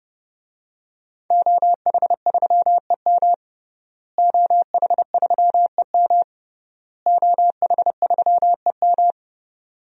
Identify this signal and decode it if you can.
Morse code, heard as a keyed tone, about 30 wpm, 700 Hz. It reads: OH3EM OH3EM OH3EM
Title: DXCC-100 Call Signs - 5 Characters - Repeated 3x - 30wpm